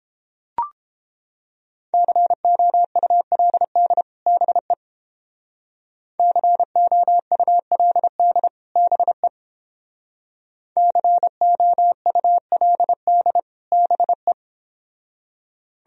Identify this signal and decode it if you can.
Morse code, heard as a keyed tone, about 31 wpm, 700 Hz. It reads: COULD BE COULD BE COULD BE